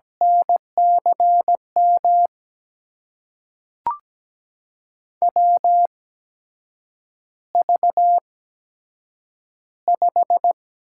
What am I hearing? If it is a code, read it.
Morse code, heard as a keyed tone, about 17 wpm, spaced as for 7 wpm, 700 Hz. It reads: NCM WV5
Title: USA Call Signs - 8x Character Spacing - 17wpm